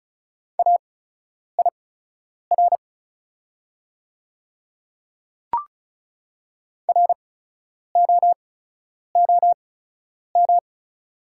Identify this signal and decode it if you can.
Morse code, heard as a keyed tone, about 35 wpm, spaced as for 10 wpm, 690 Hz. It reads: AIR ROOM
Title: 8x Character Spacing - Top 300 Words - 35wpm